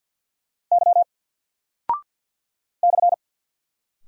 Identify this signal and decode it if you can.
Morse code, heard as a keyed tone, about 48 wpm, 700 Hz. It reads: Y /
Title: Single Character (Letter:Number:Punctuation:Prosign) 50wpm